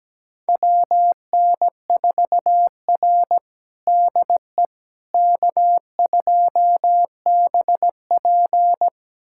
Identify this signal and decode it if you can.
Morse code, heard as a keyed tone, about 17 wpm, 700 Hz. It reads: WN4R DE K2BP
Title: USA Callsign DE Callsign - 4 Characters - 17wpm